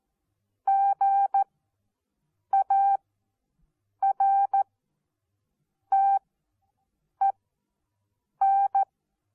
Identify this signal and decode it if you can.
Morse code, heard as a keyed tone, about 14 wpm, spaced as for 6 wpm, 780 Hz. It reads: GARTEN
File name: Onlinelehrgang / morsen / DJ4UF Morsen14